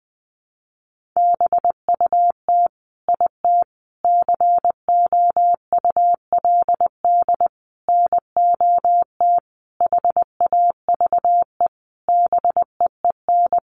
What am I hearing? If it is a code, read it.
Morse code, heard as a keyed tone, about 20 wpm, 700 Hz. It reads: BUT IT COULD NOT HAVE BEEN
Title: Sentences from Top 1400 Words 20wpm